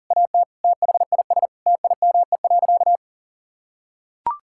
Morse code, heard as a keyed tone, about 40 wpm, 700 Hz. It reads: AT THIS TIME.
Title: Sentences from Top 1000 Words - Review Part 5 - 40wpm